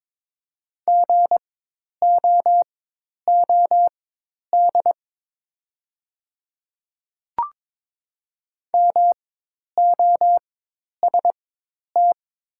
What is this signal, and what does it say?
Morse code, heard as a keyed tone, about 22 wpm, spaced as for 10 wpm, 700 Hz. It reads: GOOD MOST
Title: 4x Character Spacing - Top 200 Words - 22wpm